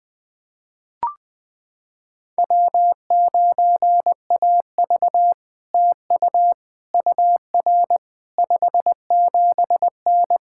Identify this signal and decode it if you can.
Morse code, heard as a keyed tone, about 20 wpm, 700 Hz. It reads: W9AV TU UR 57N
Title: POTA QSO Practice - 20wpm